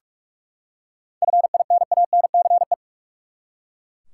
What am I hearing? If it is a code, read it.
Morse code, heard as a keyed tone, about 42 wpm, 700 Hz. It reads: FINANCE
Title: Seven Letter Words - 45wpm